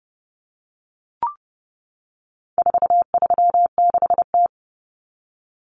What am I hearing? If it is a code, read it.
Morse code, heard as a keyed tone, about 30 wpm, 700 Hz. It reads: 436T
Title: Common Cut Numbers - 4-Digit 30wpm